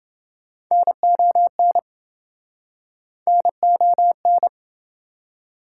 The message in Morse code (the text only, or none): NON NON